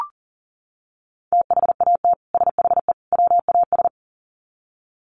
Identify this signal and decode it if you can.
Morse code, heard as a keyed tone, about 40 wpm, 700 Hz. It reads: THAT SHE WAS
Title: Sets of 3 Words 40wpm